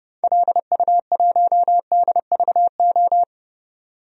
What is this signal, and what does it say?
Morse code, heard as a keyed tone, about 31 wpm, 700 Hz. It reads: LU1DVO
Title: DXCC-100 Call Signs - 6 Characters - 30wpm